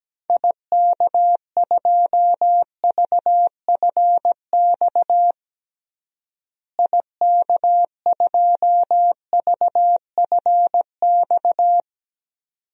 Morse code, heard as a keyed tone, about 17 wpm, 700 Hz. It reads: IK2VFX IK2VFX
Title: DXCC-100 Call Signs - 6 Characters - Repeated 2x - 17wpm